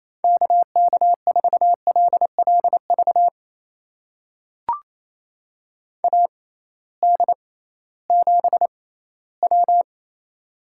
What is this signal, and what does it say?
Morse code, heard as a keyed tone, about 28 wpm, spaced as for 16 wpm, 700 Hz. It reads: KK4LLV AD7W
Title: USA Call Signs - 6x Character Spacing - 28wpm